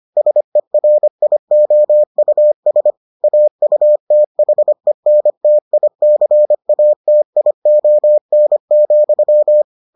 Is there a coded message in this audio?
Yes: Morse code from 0.2 to 9.6 s, heard as a keyed tone, about 25 wpm, 600 Hz. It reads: SERIOUS AUTHENTICATION,